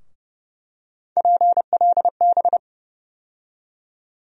Morse code, heard as a keyed tone, about 30 wpm, 700 Hz. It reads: PLB
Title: USA Call Sign Postfixes - 3 Letters 30wpm